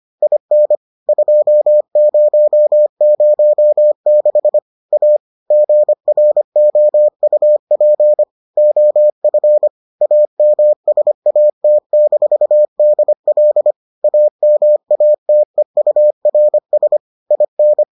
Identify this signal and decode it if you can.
Morse code, heard as a keyed tone, about 25 wpm, 600 Hz. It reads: IN 2006 A GROUP OF AMSAT-DL AMATEURS IN